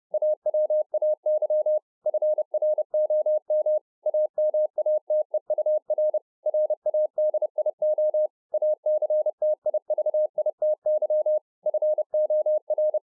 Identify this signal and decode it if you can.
Morse code, heard as a keyed tone, about 30 wpm, 610 Hz. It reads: AWAY FROM AMATEUR RADIO ACTIVITY FOR